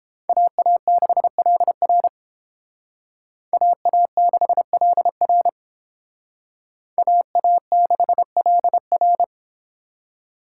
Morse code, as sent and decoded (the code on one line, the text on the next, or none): .- .- -.... .-.. .-. / .- .- -.... .-.. .-. / .- .- -.... .-.. .-.
AA6LR AA6LR AA6LR